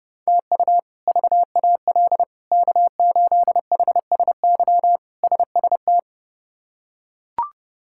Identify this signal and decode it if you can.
Morse code, heard as a keyed tone, about 30 wpm, 700 Hz. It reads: TU VAL K8HSY SST
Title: K1USN SST QSO Practice - 30wpm